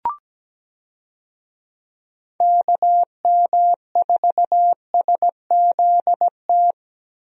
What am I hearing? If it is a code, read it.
Morse code, heard as a keyed tone, about 17 wpm, 700 Hz. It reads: KM4SZT
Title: USA Call Signs - 6 Characters - Repeated 3x - 17wpm